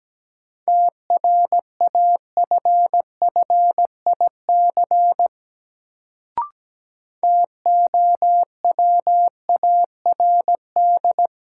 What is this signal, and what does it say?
Morse code, heard as a keyed tone, about 17 wpm, 700 Hz. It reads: TRAFFIC TOWARD
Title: Sentences from Top 1300 Words - New Word Drill Alt - 17wpm